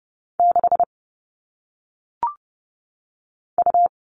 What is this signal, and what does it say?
Morse code, heard as a keyed tone, about 30 wpm, 700 Hz. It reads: 6 U
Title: All Cut Numbers - 1-Digit 30wpm